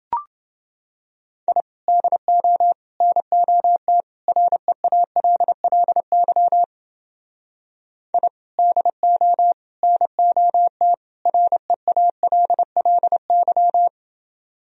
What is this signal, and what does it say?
Morse code, heard as a keyed tone, about 29 wpm, 700 Hz. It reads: I DO NOT REALLY I DO NOT REALLY